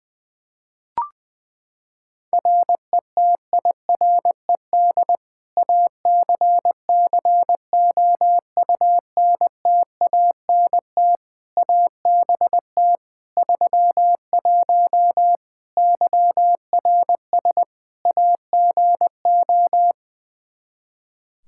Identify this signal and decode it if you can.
Morse code, heard as a keyed tone, about 21 wpm, 700 Hz. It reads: RETIRED ACCOUNTANT ABT 31 YRS AGO
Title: K1USN SST QSO Practice 20wpm